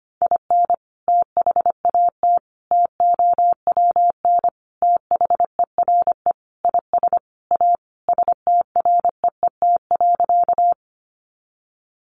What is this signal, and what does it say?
Morse code, heard as a keyed tone, about 25 wpm, 700 Hz. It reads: IN THAT TOWN THERE IS A STREET.
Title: Sentences from Top 500 Words 25wpm